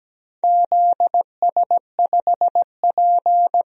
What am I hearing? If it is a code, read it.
Morse code, heard as a keyed tone, about 17 wpm, 700 Hz. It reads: ZS5P
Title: DXCC-100 Call Signs - 4 Characters - 17wpm